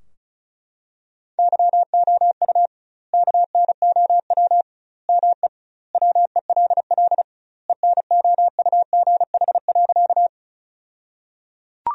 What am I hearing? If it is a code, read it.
Morse code, heard as a keyed tone, about 34 wpm, 700 Hz. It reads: YOU KNOW ME WELL ENOUGH.
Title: 2x Word-Spacing - Sentences from Top 400 Words - 35wpm